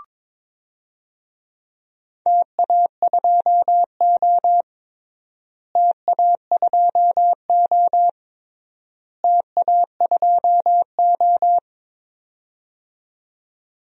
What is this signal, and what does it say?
Morse code, heard as a keyed tone, about 22 wpm, 700 Hz. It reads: TA2O TA2O TA2O
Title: DXCC-100 Call Signs - 4 Characters - Repeated 3x - 22wpm